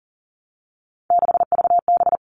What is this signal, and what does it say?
Morse code, heard as a keyed tone, about 40 wpm, 700 Hz. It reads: 6VB
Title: All Cut Numbers - 3-Digit 40wpm